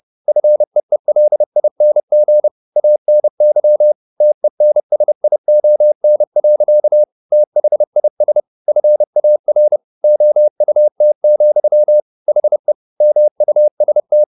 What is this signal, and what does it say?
Morse code, heard as a keyed tone, about 30 wpm, 600 Hz. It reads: FEELING ANY TENSION. THIS FAR OUT, HE MUST
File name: the old man and the sea / the old man and the sea 1 30wpm